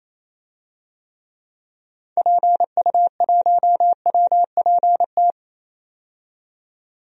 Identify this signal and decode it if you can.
Morse code, heard as a keyed tone, about 28 wpm, 700 Hz. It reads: PU1WPT
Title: DXCC-100 Call Signs - 6 Characters - 28wpm